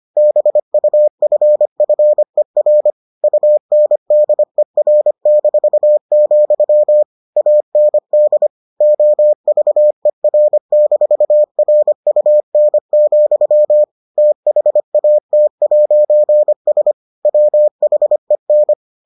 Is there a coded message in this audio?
Yes: Morse code from 0.2 to 18.7 s, heard as a keyed tone, about 25 wpm, 600 Hz. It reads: BUFFER UNDER-, AND OVER-RUN, THAT'S WHEN